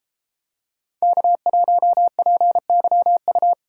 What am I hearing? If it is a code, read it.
Morse code, heard as a keyed tone, about 31 wpm, 700 Hz. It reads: K1PYU